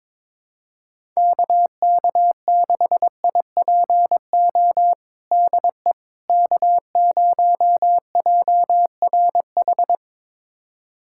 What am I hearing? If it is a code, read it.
Morse code, heard as a keyed tone, about 21 wpm, 700 Hz. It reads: KK6IPO DE K0JRH
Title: USA Callsign DE Callsign 22wpm